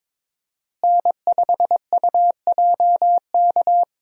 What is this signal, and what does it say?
Morse code, heard as a keyed tone, about 21 wpm, 700 Hz. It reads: N5UJK
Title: USA Call Signs - 5 Characters - Repeated 3x - 22wpm